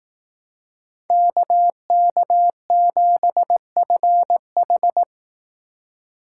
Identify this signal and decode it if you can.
Morse code, heard as a keyed tone, about 19 wpm, 700 Hz. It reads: KK7FH